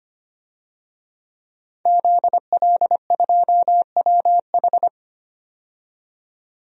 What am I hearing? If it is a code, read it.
Morse code, heard as a keyed tone, about 25 wpm, 700 Hz. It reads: ZL2WH
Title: DXCC-100 Call Signs - 5 Characters - 25wpm